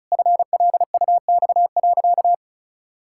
Morse code, heard as a keyed tone, about 36 wpm, 700 Hz. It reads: FLUX.
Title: Pangram Challenge v2 - 35wpm